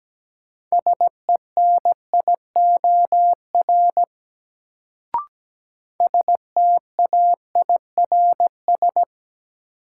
Morse code, heard as a keyed tone, about 17 wpm, 700 Hz. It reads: SENIOR STAIRS
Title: Sets of 5 Words Encore - New Word Drill - 17wpm